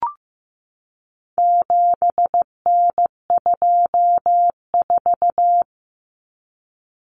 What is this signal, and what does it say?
Morse code, heard as a keyed tone, about 15 wpm, 700 Hz. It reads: 7N24
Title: Common Cut Numbers - 4-Digit 15wpm